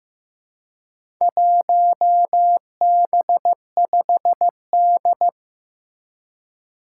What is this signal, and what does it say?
Morse code, heard as a keyed tone, about 15 wpm, 700 Hz. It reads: 1B5D
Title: Four Challenging Letter-Number Combinations 15wpm